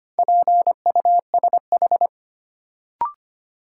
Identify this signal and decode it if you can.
Morse code, heard as a keyed tone, about 25 wpm, 700 Hz. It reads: PUSH
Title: Sentences from Top 1300 Words - New Word Drill - 25wpm